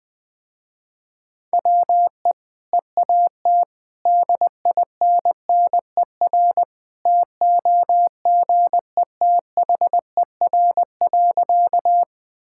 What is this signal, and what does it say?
Morse code, heard as a keyed tone, about 20 wpm, 700 Hz. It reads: WE EAT DINNER TOGETHER.
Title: Easy Sentences with Hints - 20wpm